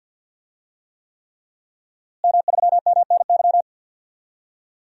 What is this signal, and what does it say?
Morse code, heard as a keyed tone, about 48 wpm, 700 Hz. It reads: M3MNY
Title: DXCC-100 Call Signs - 5 Characters - Repeated 3x - 50wpm